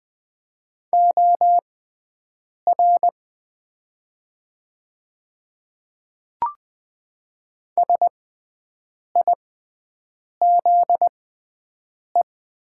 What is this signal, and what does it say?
Morse code, heard as a keyed tone, about 20 wpm, spaced as for 7 wpm, 700 Hz. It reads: OR SIZE
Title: 6x Character Spacing - Top 500 Words - 20wpm